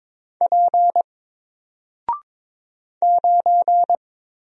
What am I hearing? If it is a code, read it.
Morse code, heard as a keyed tone, about 22 wpm, 700 Hz. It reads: P 9
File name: Single-Letter-Number-22wpm